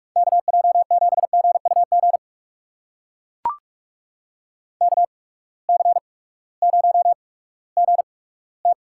Morse code, heard as a keyed tone, about 43 wpm, spaced as for 20 wpm, 700 Hz. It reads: KJ7GUG KC0GT
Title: USA Call Signs - 6 Characters - 8x Character Spacing - 45wpm